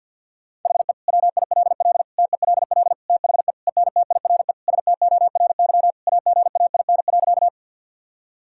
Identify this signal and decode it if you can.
Morse code, heard as a keyed tone, about 50 wpm, 700 Hz. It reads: HE WILL TELL THE ENTIRE STORY AGAIN.
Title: Sentences from Top 1000 Words - Review Part 5 - 50wpm